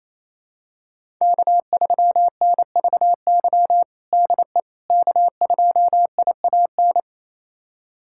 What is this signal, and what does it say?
Morse code, heard as a keyed tone, about 28 wpm, 700 Hz. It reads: K3NVY DE K2IAN
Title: USA Callsign DE Callsign - 5 Characters - 28wpm